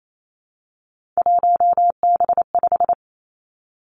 Morse code, heard as a keyed tone, about 28 wpm, 690 Hz. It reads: 1B5
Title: Three Challenging Letter-Number Combinations 28wpm